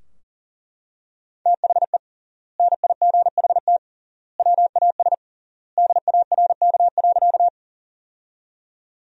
Morse code, heard as a keyed tone, about 40 wpm, 700 Hz. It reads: THE NIGHT WAS DARK.
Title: 3x Word-Spacing - Sentences from Top 500 Words - 40wpm